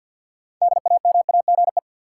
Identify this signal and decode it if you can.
Morse code, heard as a keyed tone, about 53 wpm, 700 Hz. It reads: DAMAGE
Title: Six Letter Words - 50wpm